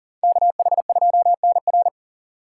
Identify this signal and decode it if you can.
Morse code, heard as a keyed tone, about 52 wpm, 700 Hz. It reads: KH2NR